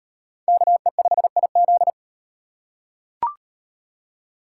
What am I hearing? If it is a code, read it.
Morse code, heard as a keyed tone, about 38 wpm, 700 Hz. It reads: KE5IZ